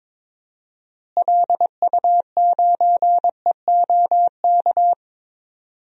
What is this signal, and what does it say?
Morse code, heard as a keyed tone, about 22 wpm, 700 Hz. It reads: LU9EOK
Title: DXCC-100 Call Signs - 6 Characters - Repeated 2x - 22wpm